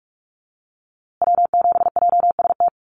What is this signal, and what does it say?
Morse code, heard as a keyed tone, about 44 wpm, 700 Hz. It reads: W7JST
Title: USA Call Signs Repeated 2x 45wpm